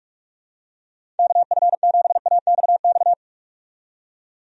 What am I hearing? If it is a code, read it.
Morse code, heard as a keyed tone, about 46 wpm, 700 Hz. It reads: KF7AXX
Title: USA Call Signs - 6 Characters - 45wpm